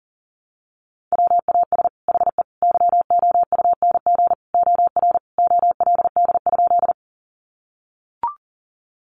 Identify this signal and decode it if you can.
Morse code, heard as a keyed tone, about 40 wpm, 700 Hz. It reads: WAS HE YOUNG OR OLD?
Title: Sentences from Top 400 Words 40wpm